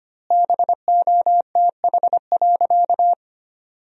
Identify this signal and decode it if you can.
Morse code, heard as a keyed tone, about 25 wpm, 700 Hz. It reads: BOTH.
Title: 4x Word-Spacing - Sentences from Top 300 Words - 25wpm